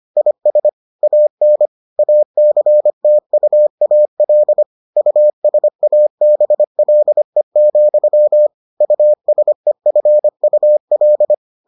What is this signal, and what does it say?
Morse code, heard as a keyed tone, about 25 wpm, 600 Hz. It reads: IS AN ACTUAL USABLE, USEFUL